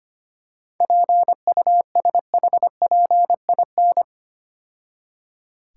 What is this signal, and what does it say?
Morse code, heard as a keyed tone, about 26 wpm, 700 Hz. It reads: PUSHPIN
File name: Easy Words - Speed-Racing with Hints - 25wpm